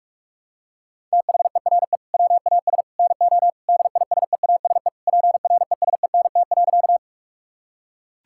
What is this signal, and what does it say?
Morse code, heard as a keyed tone, about 46 wpm, 700 Hz. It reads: THERE WAS NO DISEASE PRESENT.
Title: Sentences from Top 1000 Words - Review Part 6 - 45wpm